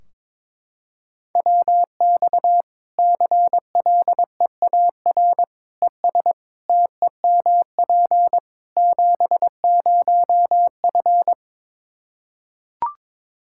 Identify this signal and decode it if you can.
Morse code, heard as a keyed tone, about 22 wpm, 700 Hz. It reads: WX CLEAR ES TEMP 70F
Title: K1USN SST QSO Practice 22wpm